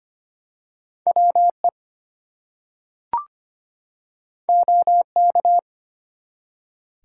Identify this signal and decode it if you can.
Morse code, heard as a keyed tone, about 25 wpm, 700 Hz. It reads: WE OK
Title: Two Letter Words - 25wpm